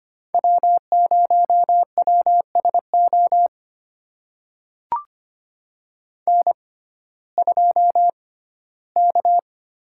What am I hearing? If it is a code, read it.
Morse code, heard as a keyed tone, about 25 wpm, spaced as for 15 wpm, 700 Hz. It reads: W0WSO N2K
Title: USA Call Signs - 5 Characters - 6x Character Spacing - 25wpm